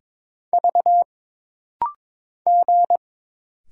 Morse code, heard as a keyed tone, about 21 wpm, 700 Hz. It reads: V G
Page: This rapid-fire timing is geared to push your ICR (Instant Character Recognition) skills.